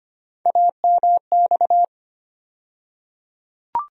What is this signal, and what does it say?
Morse code, heard as a keyed tone, about 25 wpm, 700 Hz. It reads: AMX
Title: USA Call Sign Postfixes - 3 Letters 25wpm